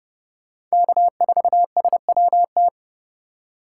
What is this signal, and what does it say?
Morse code, heard as a keyed tone, about 39 wpm, 700 Hz. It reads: K4SWT